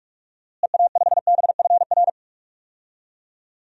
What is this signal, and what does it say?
Morse code, heard as a keyed tone, about 46 wpm, 700 Hz. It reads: EA5BFR
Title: DXCC-100 Call Signs - 6 Characters - Repeated 3x - 45wpm